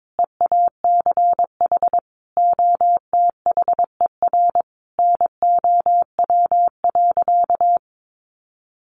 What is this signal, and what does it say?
Morse code, heard as a keyed tone, about 22 wpm, 700 Hz. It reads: EACH OTHER NOW.
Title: Sentences from Top 1200 Words 22wpm